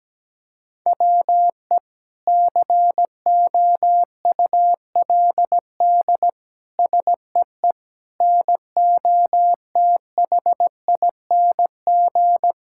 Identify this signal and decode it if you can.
Morse code, heard as a keyed tone, about 17 wpm, 700 Hz. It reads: WE COULD SEE NOTHING
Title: Sentences from Top 1000 Words - Review Part 5 - 17wpm